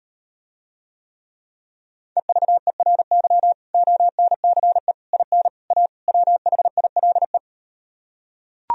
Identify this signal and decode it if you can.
Morse code, heard as a keyed tone, about 38 wpm, 700 Hz. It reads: EVERY ONCE IN A WHILE